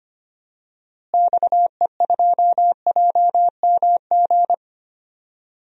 Morse code, heard as a keyed tone, about 25 wpm, 700 Hz. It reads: XE2JMG
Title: DXCC-100 Call Signs - 6 Characters - Repeated 2x - 25wpm